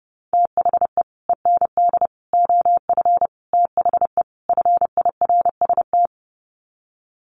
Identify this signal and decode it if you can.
Morse code, heard as a keyed tone, about 30 wpm, 700 Hz. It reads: THE END OF THE FIRST
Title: Sets of 5 Words Encore 30wpm